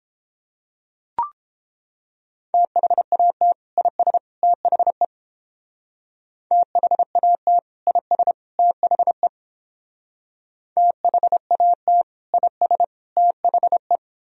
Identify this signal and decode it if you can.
Morse code, heard as a keyed tone, about 30 wpm, 700 Hz. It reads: THAT IS THE THAT IS THE THAT IS THE